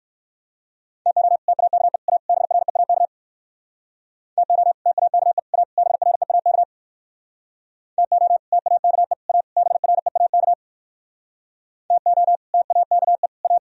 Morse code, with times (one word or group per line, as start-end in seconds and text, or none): 1.1-1.4 TO
1.5-2.0 TAKE
2.1-2.2 A
2.3-3.1 BREAK
4.4-4.7 TO
4.9-5.4 TAKE
5.5-5.6 A
5.8-6.6 BREAK
8.0-8.4 TO
8.5-9.1 TAKE
9.3-9.4 A
9.6-10.5 BREAK
11.9-12.4 TO
12.5-13.3 TAKE
13.4-13.6 A